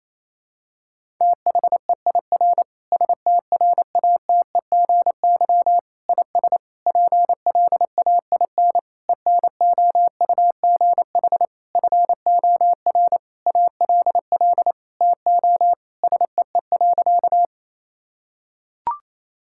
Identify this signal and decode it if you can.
Morse code, heard as a keyed tone, about 28 wpm, 700 Hz. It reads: THEIR STRATEGY IS PLAIN ENOUGH FOR ALL TO SEE.
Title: Sentences from Top 1000 Words - Review Part 5 - 28wpm